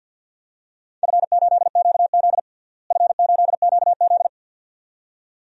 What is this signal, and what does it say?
Morse code, heard as a keyed tone, about 52 wpm, 700 Hz. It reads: F8QZ F8QZ
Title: DXCC-100 Call Signs - 4 Characters - Repeated 2x - 50wpm